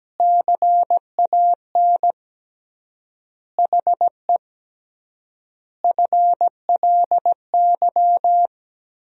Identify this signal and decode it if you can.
Morse code, heard as a keyed tone, about 17 wpm, 700 Hz. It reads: CAN HE FLY
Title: 3x Word-Spacing - Sentences from Top 500 Words - 17wpm